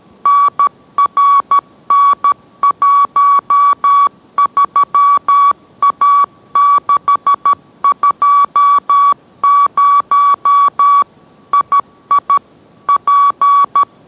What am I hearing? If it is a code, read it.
Morse code, heard as a keyed tone, about 13 wpm, 1140 Hz. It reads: NRN13A620 II P